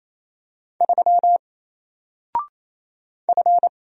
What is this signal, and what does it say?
Morse code, heard as a keyed tone, about 27 wpm, 700 Hz. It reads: 3 F
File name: Single-Letter-Number-28wpm